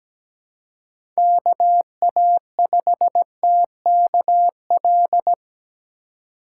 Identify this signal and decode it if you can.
Morse code, heard as a keyed tone, about 17 wpm, 700 Hz. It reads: KA5TKL
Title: USA Call Signs - 4x Character Spacing - 17wpm